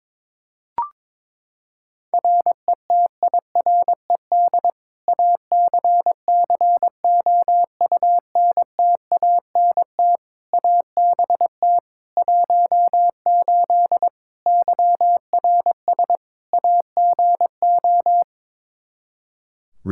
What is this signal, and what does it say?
Morse code, heard as a keyed tone, about 21 wpm, 700 Hz. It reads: RETIRED ACCOUNTANT ABT 18 YRS AGO
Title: K1USN SST QSO Practice 22wpm